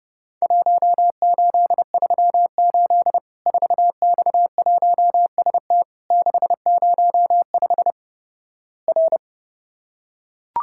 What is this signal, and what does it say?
Morse code, heard as a keyed tone, about 32 wpm, 700 Hz. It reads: 1838 4X1ST 605 R
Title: Worked All Europe DX Contest - QTC Exchange - 30wpm